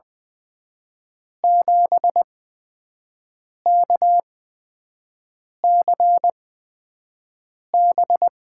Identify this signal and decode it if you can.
Morse code, heard as a keyed tone, about 20 wpm, spaced as for 5 wpm, 700 Hz. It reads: 7KCB
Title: USA Call Signs - 5 Characters - 8x Character Spacing - 20wpm